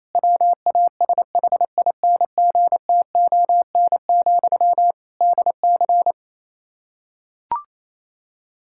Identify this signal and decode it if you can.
Morse code, heard as a keyed tone, about 28 wpm, 700 Hz. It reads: WASHINGTON, DC
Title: K1USN SST QSO Practice 28wpm